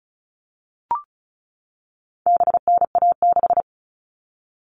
Morse code, heard as a keyed tone, about 34 wpm, 700 Hz. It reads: BNA6
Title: All Cut Numbers - 4-Digit 35wpm